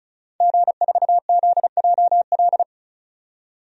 Morse code, heard as a keyed tone, about 36 wpm, 700 Hz. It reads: G4ZJL
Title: DXCC-100 Call Signs - 5 Characters - 35wpm